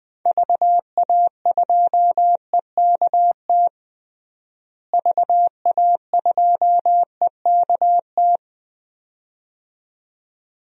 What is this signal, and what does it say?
Morse code, heard as a keyed tone, about 20 wpm, 700 Hz. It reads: VA2EKT VA2EKT
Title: DXCC-100 Call Signs - 6 Characters - Repeated 2x - 20wpm